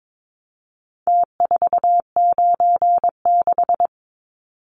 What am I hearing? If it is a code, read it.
Morse code, heard as a keyed tone, about 22 wpm, 700 Hz. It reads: T496
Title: Numbers - Contest Style - 4-Digit 22wpm